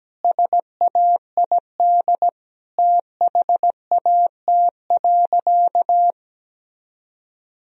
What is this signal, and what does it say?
Morse code, heard as a keyed tone, about 17 wpm, 700 Hz. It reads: SAID THAT.
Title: Sentences from Top 1000 Words - Review Part 1 - 17wpm